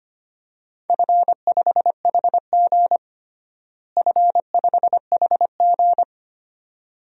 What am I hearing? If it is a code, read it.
Morse code, heard as a keyed tone, about 25 wpm, 700 Hz. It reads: F5HG F5HG
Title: DXCC-100 Call Signs - 4 Characters - Repeated 2x - 25wpm